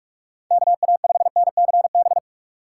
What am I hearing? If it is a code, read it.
Morse code, heard as a keyed tone, about 46 wpm, 700 Hz. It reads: KA5NCB
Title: USA Call Signs - 2x Character Spacing - 45wpm